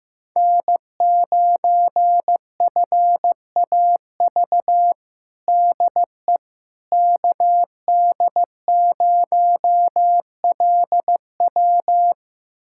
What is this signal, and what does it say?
Morse code, heard as a keyed tone, about 15 wpm, 700 Hz. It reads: N9FAV DE KD0LW
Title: USA Callsign DE Callsign - 5 Characters - 15wpm